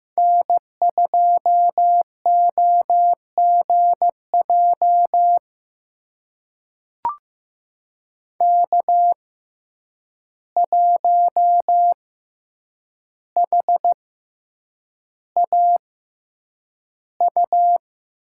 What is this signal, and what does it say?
Morse code, heard as a keyed tone, about 15 wpm, spaced as for 8 wpm, 700 Hz. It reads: N2OGJ K1HAU
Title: USA Call Signs - 5 Characters - 6x Character Spacing - 15wpm